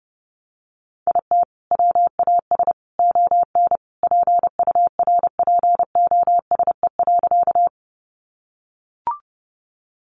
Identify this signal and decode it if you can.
Morse code, heard as a keyed tone, about 30 wpm, 700 Hz. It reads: IT WAS ON PURPOSE.
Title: Sentences from Top 800 Words 30wpm